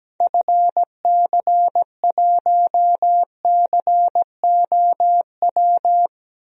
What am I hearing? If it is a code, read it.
Morse code, heard as a keyed tone, about 17 wpm, 700 Hz. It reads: FC1COW
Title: DXCC-100 Call Signs - 6 Characters - Repeated 3x - 17wpm